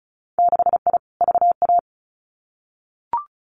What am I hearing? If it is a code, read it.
Morse code, heard as a keyed tone, about 36 wpm, 700 Hz. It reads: 6I VA
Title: Winter Field Day Exchanges Encore 35wpm